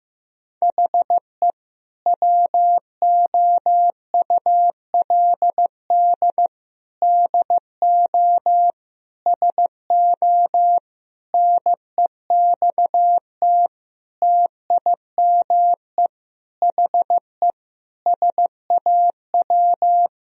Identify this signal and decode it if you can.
Morse code, heard as a keyed tone, about 15 wpm, 700 Hz. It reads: HE WOULD DO SO NEXT TIME HE SAW
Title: Sentences from Top 1000 Words - Review Part 1 - 15wpm